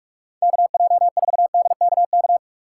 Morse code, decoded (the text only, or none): KJ4DKK